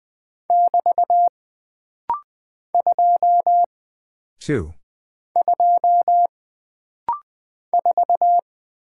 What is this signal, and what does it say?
Morse code, heard as a keyed tone, about 20 wpm, 700 Hz. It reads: <BT> 2 2 4
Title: Single Character (Letter:Number:Punctuation:Prosign) - Rapid-Fire - 20wpm